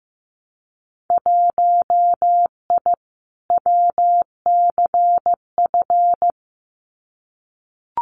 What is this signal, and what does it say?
Morse code, heard as a keyed tone, about 15 wpm, 700 Hz. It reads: 1I WCF
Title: Winter Field Day Exchanges Encore 15wpm